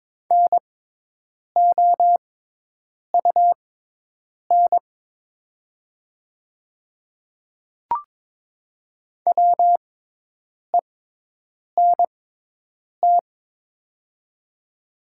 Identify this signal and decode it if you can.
Morse code, heard as a keyed tone, about 22 wpm, spaced as for 8 wpm, 700 Hz. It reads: NOUN WENT
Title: 6x Character Spacing - Top 500 Words - 22wpm